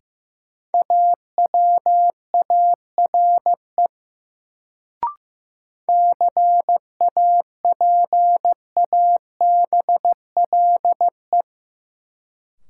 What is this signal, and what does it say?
Morse code, heard as a keyed tone, about 15 wpm, 700 Hz. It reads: AWARE CAPABLE
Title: Sentences from Top 1200 Words - New Word Drill Alt - 15wpm